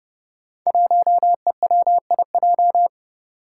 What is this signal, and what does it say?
Morse code, heard as a keyed tone, about 30 wpm, 700 Hz. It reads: 1EWIJ
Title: Five Challenging Letter-Number Combinations 30wpm